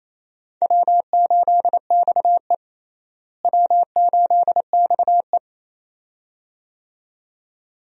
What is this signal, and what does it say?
Morse code, heard as a keyed tone, about 27 wpm, 700 Hz. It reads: W8XE W8XE
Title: USA Call Signs - 4 Characters - Repeated 2x - 28wpm